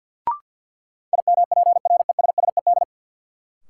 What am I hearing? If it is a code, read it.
Morse code, heard as a keyed tone, about 53 wpm, 700 Hz. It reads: IMPRESSED
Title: Sentences from Top 1600 Words - New Word Drill Alt - 50wpm